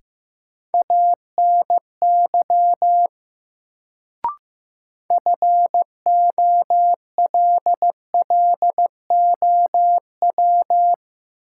Morse code, heard as a keyed tone, about 15 wpm, 700 Hz. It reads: ANY FOLLOW
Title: Sentences from Top 200 Words - New Word Drill - 15wpm